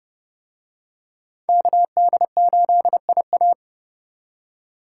Morse code, heard as a keyed tone, about 29 wpm, 700 Hz. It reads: KD8IA